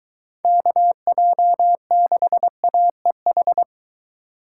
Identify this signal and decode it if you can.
Morse code, heard as a keyed tone, about 23 wpm, 700 Hz. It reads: KJ6AEH